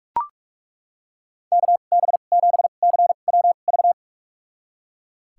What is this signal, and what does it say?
Morse code, heard as a keyed tone, about 41 wpm, spaced as for 31 wpm, 700 Hz. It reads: KB7CWV
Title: USA Call Signs - 6 Characters - 2x Character Spacing - 45wpm